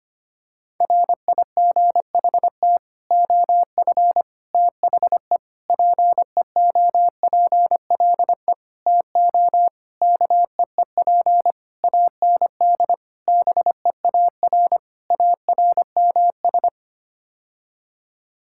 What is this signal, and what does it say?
Morse code, heard as a keyed tone, about 25 wpm, 700 Hz. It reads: RIGHT OF THE PEOPLE TO KEEP AND BEAR ARMS
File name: Sets of 9 Words - 25wpm